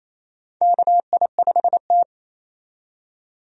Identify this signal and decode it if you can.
Morse code, heard as a keyed tone, about 28 wpm, 700 Hz. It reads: KI5T
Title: USA Call Signs - 4 Characters - 28wpm